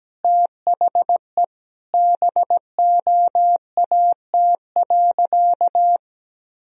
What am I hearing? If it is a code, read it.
Morse code, heard as a keyed tone, about 17 wpm, 700 Hz. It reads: THE BOAT.
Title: Sentences from Top 1000 Words - Review Part 3 - 17wpm